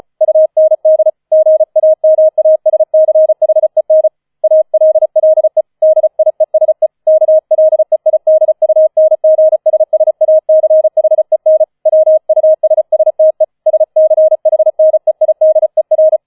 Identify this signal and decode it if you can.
Morse code, heard as a keyed tone, about 34 wpm, 630 Hz. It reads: UND GAMASCHEN ALLE DIESE KLEIDUNGSSACHEN WUSSTE SCHNEIDER